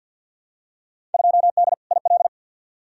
Morse code, heard as a keyed tone, about 47 wpm, 700 Hz. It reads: 2D IL
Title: Field Day Exchanges - 50wpm